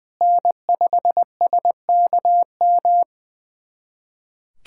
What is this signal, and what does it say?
Morse code, heard as a keyed tone, about 21 wpm, 700 Hz. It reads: N5SKM
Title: USA Callsign DE Callsign 20wpm